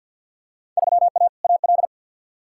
Morse code, heard as a keyed tone, about 47 wpm, 700 Hz. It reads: 3A AL
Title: Field Day Exchanges - 50wpm